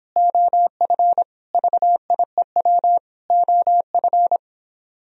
Morse code, heard as a keyed tone, about 29 wpm, 700 Hz. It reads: OF VIEW OF